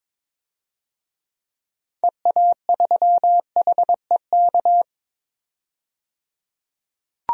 Morse code, heard as a keyed tone, about 22 wpm, 700 Hz. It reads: EA3HEK
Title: DXCC-100 Call Signs - 6 Characters - 22wpm